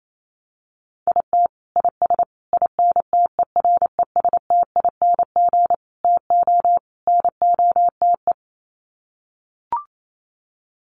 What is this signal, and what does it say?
Morse code, heard as a keyed tone, about 28 wpm, 700 Hz. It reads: IT IS INTERESTING TO NOTE
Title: Sets of 5 Words Encore2 28wpm